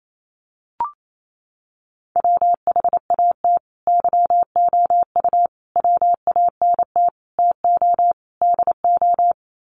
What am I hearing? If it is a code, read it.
Morse code, heard as a keyed tone, about 28 wpm, 700 Hz. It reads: WHAT YOU WANT TO DO
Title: Sets of 5 Words Encore 28wpm